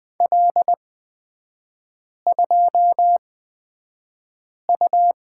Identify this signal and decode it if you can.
Morse code, heard as a keyed tone, about 20 wpm, spaced as for 5 wpm, 700 Hz. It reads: L2U
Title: 255 Lesson 41e - Period - Three Characters 20wpm